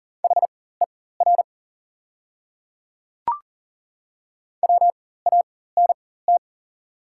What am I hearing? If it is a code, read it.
Morse code, heard as a keyed tone, about 39 wpm, spaced as for 19 wpm, 700 Hz. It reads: HER WANT
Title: 4x Character Spacing - Top 200 Words - 40wpm